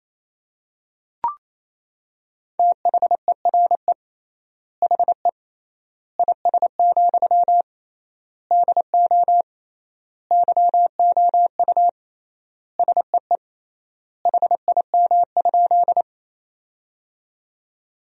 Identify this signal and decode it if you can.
Morse code, heard as a keyed tone, about 27 wpm, 700 Hz. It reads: THERE HE IS, DO YOU SEE HIM?
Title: 3x Word-Spacing - Sentences from Top 100 Words - 28wpm